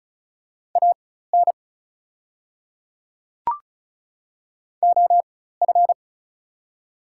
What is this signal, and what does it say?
Morse code, heard as a keyed tone, about 36 wpm, spaced as for 16 wpm, 700 Hz. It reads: AN OF
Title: 4x Character Spacing - Top 300 Words - 35wpm